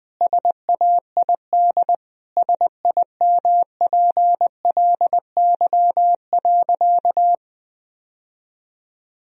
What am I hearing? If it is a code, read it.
Morse code, heard as a keyed tone, about 20 wpm, 700 Hz. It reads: SAID SIMPLY.
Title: Sentences from Top 1000 Words - Review Part 4 - 20wpm